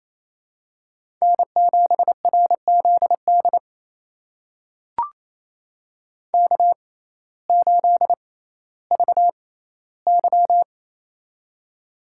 Morse code, heard as a keyed tone, about 28 wpm, spaced as for 15 wpm, 700 Hz. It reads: N7RZD K8VY
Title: USA Call Signs - 6x Character Spacing - 28wpm